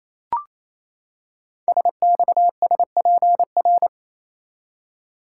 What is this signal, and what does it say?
Morse code, heard as a keyed tone, about 28 wpm, 700 Hz. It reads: SXSPR
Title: Code Groups - 5 Letters - 28wpm